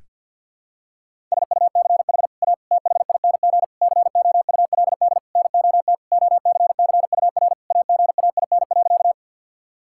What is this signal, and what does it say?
Morse code, heard as a keyed tone, about 52 wpm, 700 Hz. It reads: SUCH A THING COULD NOT OCCUR AGAIN.
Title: Sentences from Top 1000 Words - Review Part 6 - 50wpm